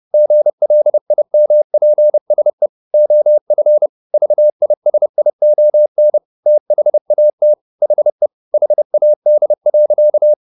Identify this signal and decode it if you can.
Morse code, heard as a keyed tone, about 30 wpm, 600 Hz. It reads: GLIMPSE OF VISION THAT HE HAD.